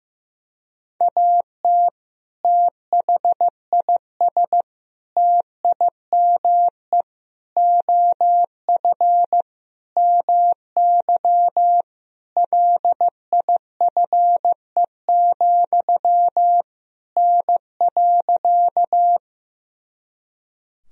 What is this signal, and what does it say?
Morse code, heard as a keyed tone, about 15 wpm, 700 Hz. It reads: AT THIS TIME OF MY LIFE, N.
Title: Sentences from Top 300 Words and Abbreviations 15wpm